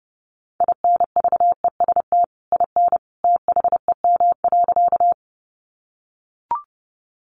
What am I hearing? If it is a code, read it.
Morse code, heard as a keyed tone, about 30 wpm, 700 Hz. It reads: INVEST IN THEM.
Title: Sentences from Top 1100 Words 30wpm